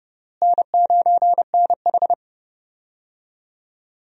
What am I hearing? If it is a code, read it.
Morse code, heard as a keyed tone, about 32 wpm, 700 Hz. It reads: N9NH
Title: USA Call Signs - 4 Characters - Repeated 3x - 30wpm